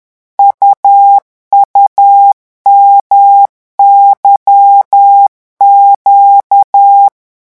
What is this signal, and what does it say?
Morse code, heard as a keyed tone, about 10 wpm, 800 Hz. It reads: UUMYQ
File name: morse mp3 / revision2-vitesse 10 mots